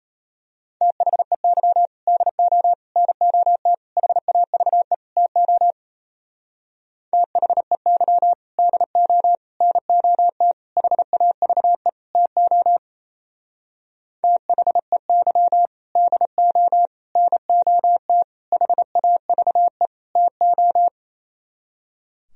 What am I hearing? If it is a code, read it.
Morse code, heard as a keyed tone, about 32 wpm, 700 Hz. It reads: THEY DO NOT HAVE TO THEY DO NOT HAVE TO THEY DO NOT HAVE TO
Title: Speed-Racing - Sets of 5 Words 28wpm